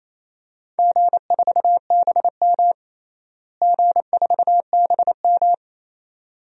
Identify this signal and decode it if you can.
Morse code, heard as a keyed tone, about 28 wpm, 700 Hz. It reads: G4BM G4BM
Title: DXCC-100 Call Signs - 4 Characters - Repeated 2x - 28wpm